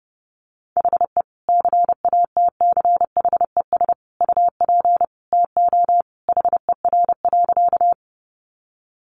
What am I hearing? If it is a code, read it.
Morse code, heard as a keyed tone, about 29 wpm, 700 Hz. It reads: HE CATCHES UP TO HER.
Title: Additional Easy Phrases 30wpm